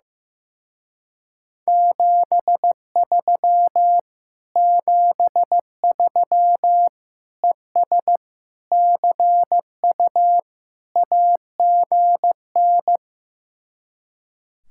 Morse code, heard as a keyed tone, about 15 wpm, 700 Hz. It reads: 73 73 ES CU AGN
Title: Speed-Racing Common QSO Phrases Encore - 15wpm